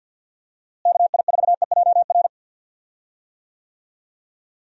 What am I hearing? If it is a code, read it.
Morse code, heard as a keyed tone, about 48 wpm, 700 Hz. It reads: KI4EJR
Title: USA Call Signs - 6 Characters - Repeated 3x - 50wpm